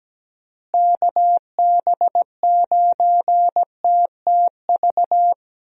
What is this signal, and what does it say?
Morse code, heard as a keyed tone, about 17 wpm, 700 Hz. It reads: KB9TTV
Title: USA Call Signs - 6 Characters - Repeated 3x - 17wpm